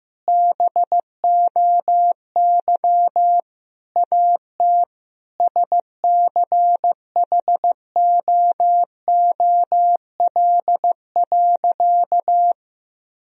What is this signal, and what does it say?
Morse code, heard as a keyed tone, about 16 wpm, 700 Hz. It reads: BOY AT SCHOOL.
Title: Sentences from Top 300 Words and Abbreviations 15wpm